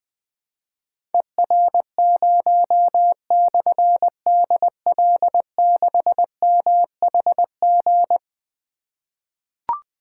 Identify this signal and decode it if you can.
Morse code, heard as a keyed tone, about 20 wpm, 700 Hz. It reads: ER0/DL6MHG
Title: Difficult Call Signs 20wpm